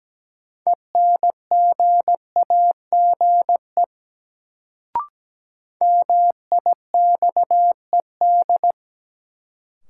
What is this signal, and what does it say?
Morse code, heard as a keyed tone, about 17 wpm, 700 Hz. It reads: ENGAGE MIXED
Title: Sentences from Top 1500 Words - New Word Drill - 17wpm